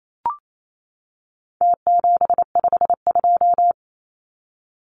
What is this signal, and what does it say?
Morse code, heard as a keyed tone, about 28 wpm, 700 Hz. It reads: T752
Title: Numbers - Contest Style - 4-Digit 28wpm